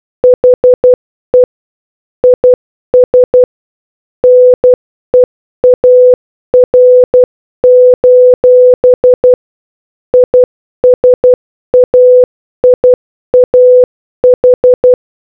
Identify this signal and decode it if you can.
Morse code, heard as a keyed tone, about 12 wpm, 500 Hz. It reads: HE IS NEAR: ISAIAH